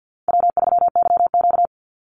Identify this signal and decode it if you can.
Morse code, heard as a keyed tone, about 50 wpm, 700 Hz. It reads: W3YQ
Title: Call Signs Repeated 2x - 50wpm